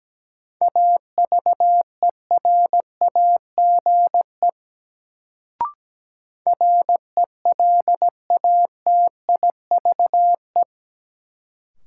Morse code, heard as a keyed tone, about 17 wpm, 700 Hz. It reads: AVERAGE RELATIVE
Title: Sentences from Top 1400 Words - New Word Drill Alt - 17wpm